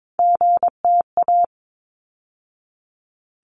Morse code, heard as a keyed tone, about 22 wpm, 700 Hz. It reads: GTA
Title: ARRL Sections 22wpm